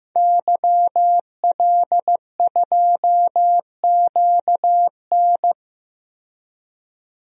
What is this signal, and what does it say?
Morse code, heard as a keyed tone, about 15 wpm, 700 Hz. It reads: YL2QN
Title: DXCC-100 Call Signs - 5 Characters - 15wpm